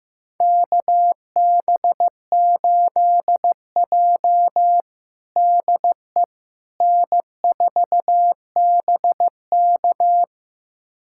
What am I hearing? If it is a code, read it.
Morse code, heard as a keyed tone, about 15 wpm, 700 Hz. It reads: KB8J DE N4BK
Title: USA Callsign DE Callsign - 4 Characters - 15wpm